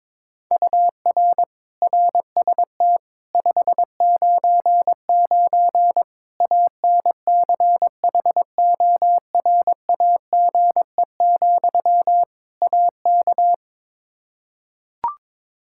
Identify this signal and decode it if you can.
Morse code, heard as a keyed tone, about 22 wpm, 700 Hz. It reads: UR RST 599 ANCHORAGE, AK
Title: K1USN SST QSO Practice 22wpm